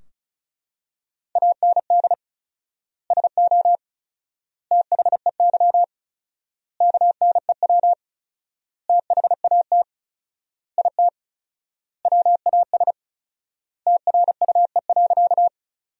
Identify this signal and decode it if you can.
Morse code, heard as a keyed tone, about 34 wpm, 700 Hz. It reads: AND SO THEY KNEW THAT IT WAS TRUE.
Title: 4x Word-Spacing - Sentences from Top 400 Words - 35wpm